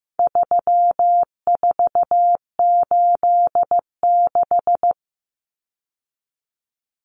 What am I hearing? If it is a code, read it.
Morse code, heard as a keyed tone, about 15 wpm, 700 Hz. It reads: 3486
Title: Common Cut Numbers - 4-Digit 15wpm